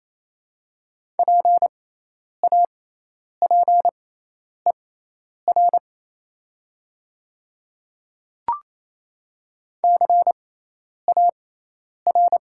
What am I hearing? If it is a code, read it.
Morse code, heard as a keyed tone, about 28 wpm, spaced as for 10 wpm, 700 Hz. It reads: PAPER CAR
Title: 6x Character Spacing - Top 300 Words - 28wpm